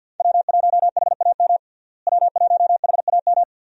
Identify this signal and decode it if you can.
Morse code, heard as a keyed tone, about 49 wpm, 700 Hz. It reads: W1HAM W1HAM
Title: USA Call Signs - 5 Characters - Repeated 2x - 50wpm